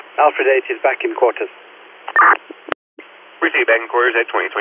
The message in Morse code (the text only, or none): none